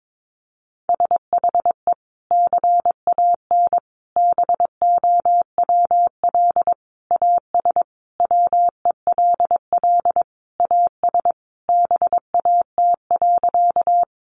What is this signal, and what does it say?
Morse code, heard as a keyed tone, about 22 wpm, 700 Hz. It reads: SHE CAN BOWL AS WELL AS BAT.
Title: Sentences from Top 1500 Words 22wpm